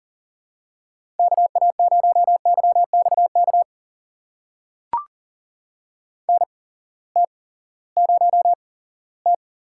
Morse code, heard as a keyed tone, about 40 wpm, spaced as for 20 wpm, 700 Hz. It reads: KA0YXK NT0T
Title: USA Call Signs - 8x Character Spacing - 40wpm